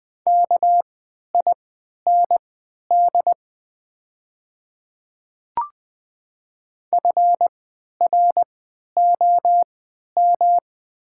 Morse code, heard as a keyed tone, about 20 wpm, spaced as for 11 wpm, 700 Hz. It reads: KIND FROM
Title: 3x Character Spacing - Top 200 Words - 20wpm